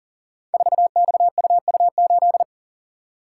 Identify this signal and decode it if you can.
Morse code, heard as a keyed tone, about 40 wpm, 700 Hz. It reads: 4XUU8
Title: Code Groups - 5 Letter:Numbers - 40wpm